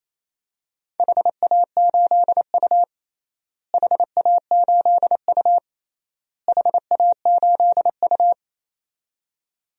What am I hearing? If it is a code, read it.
Morse code, heard as a keyed tone, about 28 wpm, 700 Hz. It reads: HA8U HA8U HA8U
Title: DXCC-100 Call Signs - 4 Characters - Repeated 3x - 28wpm